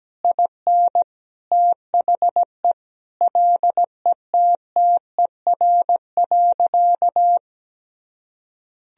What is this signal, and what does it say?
Morse code, heard as a keyed tone, about 17 wpm, 700 Hz. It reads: IN THE LETTER.
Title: Sentences from Top 1000 Words - Review Part 4 - 17wpm